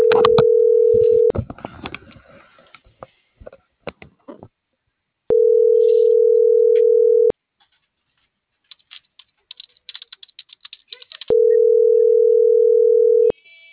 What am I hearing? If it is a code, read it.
Ringback tone (North America), 440 + 480 Hz.